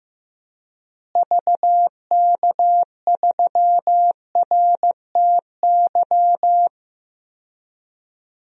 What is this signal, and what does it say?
Morse code, heard as a keyed tone, about 15 wpm, 700 Hz. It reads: VK3RTY
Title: DXCC-100 Call Signs - 6 Characters - Repeated 3x - 15wpm